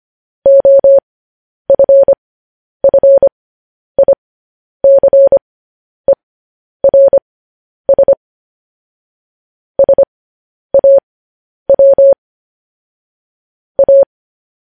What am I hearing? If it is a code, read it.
Morse code, heard as a keyed tone, about 25 wpm, spaced as for 10 wpm, 550 Hz. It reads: OFFICERS SAW A
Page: News Headlines in Morse Code at 10 WPM